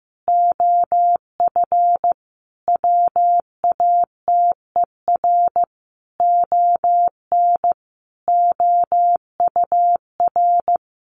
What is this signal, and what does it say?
Morse code, heard as a keyed tone, about 15 wpm, 700 Hz. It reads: OF WATER ON OUR
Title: Sentences from Top 1400 Words 15wpm